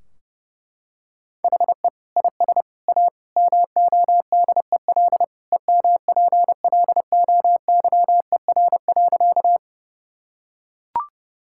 Morse code, heard as a keyed tone, about 30 wpm, 700 Hz. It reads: HE IS A MODEL EMPLOYER.
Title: Sentences from Top 1600 Words 30wpm